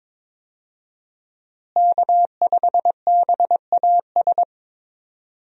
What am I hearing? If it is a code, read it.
Morse code, heard as a keyed tone, about 22 wpm, 700 Hz. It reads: K5BAS
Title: USA Call Signs - 5 Characters - Repeated 3x - 22wpm